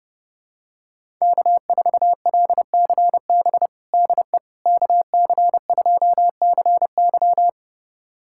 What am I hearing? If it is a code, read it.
Morse code, heard as a keyed tone, about 30 wpm, 700 Hz. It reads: K4LCB DE KC2CY
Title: USA Callsign DE Callsign - 5 Characters - 30wpm